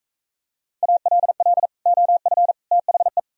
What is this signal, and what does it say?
Morse code, heard as a keyed tone, about 43 wpm, 700 Hz. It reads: ALL OF THE